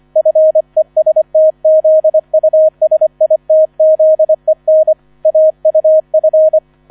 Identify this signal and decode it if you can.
Morse code, heard as a keyed tone, about 25 wpm, 630 Hz. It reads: FESTZUSITZEN AUF